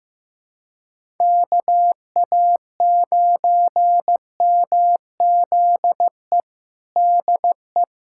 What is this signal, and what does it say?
Morse code, heard as a keyed tone, about 15 wpm, 700 Hz. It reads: KA9MZE DE
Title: USA Callsign DE Callsign 15wpm